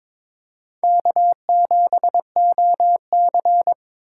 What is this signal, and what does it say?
Morse code, heard as a keyed tone, about 22 wpm, 700 Hz. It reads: K7OC
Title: USA Call Signs - 4 Characters - 4x Character Spacing - 22wpm